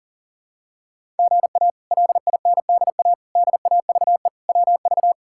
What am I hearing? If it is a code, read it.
Morse code, heard as a keyed tone, about 40 wpm, 700 Hz. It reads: GA LINDA DAVE WV
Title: K1USN SST QSO Practice - 40wpm